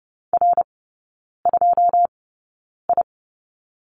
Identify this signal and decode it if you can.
Morse code, heard as a keyed tone, about 30 wpm, spaced as for 9 wpm, 700 Hz. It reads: R2I
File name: Lesson 18e - Letter m - Three Characters 30wpm